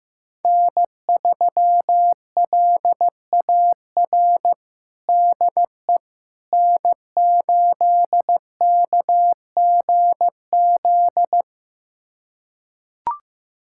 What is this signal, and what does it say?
Morse code, heard as a keyed tone, about 15 wpm, 700 Hz. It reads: N3LAR DE N8KGZ
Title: USA Callsign DE Callsign - 5 Characters - 15wpm